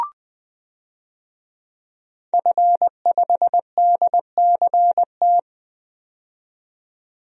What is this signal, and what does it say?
Morse code, heard as a keyed tone, about 20 wpm, 700 Hz. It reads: F5DCT
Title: DXCC-100 Call Signs - 5 Characters - 20wpm